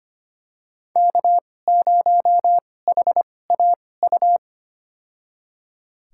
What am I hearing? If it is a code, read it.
Morse code, heard as a keyed tone, about 25 wpm, spaced as for 18 wpm, 700 Hz. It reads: K0HAU
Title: USA Call Signs - 5 Characters - 2x Character Spacing - 25wpm